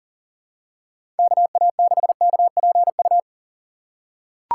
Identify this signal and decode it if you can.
Morse code, heard as a keyed tone, about 41 wpm, 700 Hz. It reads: KA6KPU
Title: USA Call Signs - 6 Characters - 6x Character Spacing - 40wpm